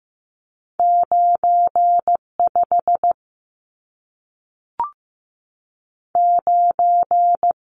Morse code, heard as a keyed tone, about 15 wpm, 700 Hz. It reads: 95 9
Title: Numbers - 2-Digit 15wpm